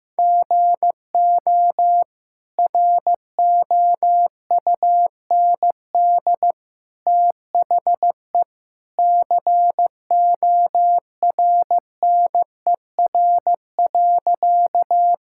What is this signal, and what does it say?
Morse code, heard as a keyed tone, about 15 wpm, 700 Hz. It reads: GO ROUND THE CORNER.
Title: Sentences from Top 1600 Words 15wpm